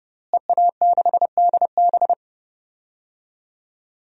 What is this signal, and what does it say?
Morse code, heard as a keyed tone, about 30 wpm, 700 Hz. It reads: EA6DB
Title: DXCC-100 Call Signs - 5 Characters - Repeated 3x - 30wpm